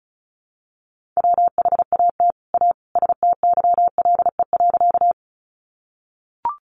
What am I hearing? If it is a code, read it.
Morse code, heard as a keyed tone, about 34 wpm, 700 Hz. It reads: WHAT A STYLE.
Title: Sentences from Top 800 Words 35wpm